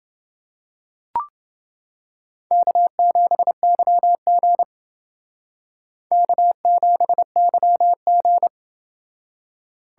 Morse code, heard as a keyed tone, about 29 wpm, 700 Hz. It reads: K7YG K7YG